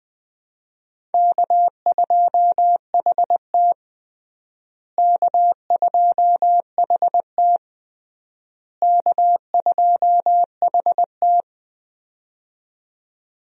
Morse code, heard as a keyed tone, about 20 wpm, 700 Hz. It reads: K2HT K2HT K2HT
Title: USA Call Signs - 4 Characters - Repeated 3x - 20wpm